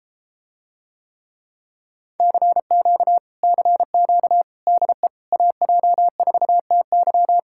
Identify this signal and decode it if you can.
Morse code, heard as a keyed tone, about 33 wpm, 700 Hz. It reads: CQ CQ DE AJ4TY